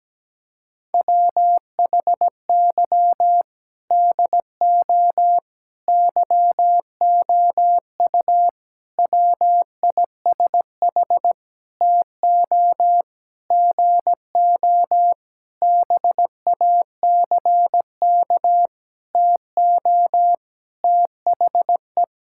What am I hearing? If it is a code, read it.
Morse code, heard as a keyed tone, about 17 wpm, 700 Hz. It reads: WHY DO YOU WISH TO GO BACK TO THE
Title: Sentences from Top 1600 Words 17wpm